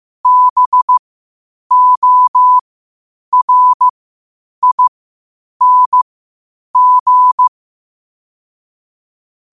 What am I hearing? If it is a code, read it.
Morse code, heard as a keyed tone, about 15 wpm, spaced as for 8 wpm, 1000 Hz. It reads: BORING
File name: qotdfiles / 8wpm